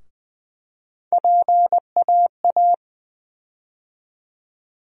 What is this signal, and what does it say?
Morse code, heard as a keyed tone, about 20 wpm, 700 Hz. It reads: PAA
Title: USA Call Sign Postfixes - 3 Letters 20wpm